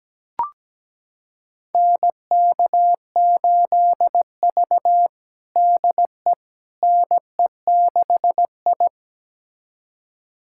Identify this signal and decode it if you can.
Morse code, heard as a keyed tone, about 17 wpm, 700 Hz. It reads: NK8V DE NE6I
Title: USA Callsign DE Callsign - 4 Characters - 17wpm